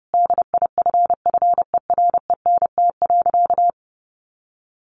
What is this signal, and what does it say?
Morse code, heard as a keyed tone, about 30 wpm, 700 Hz. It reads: DIFFERENT.
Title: Sentences from Top 1500 Words 30wpm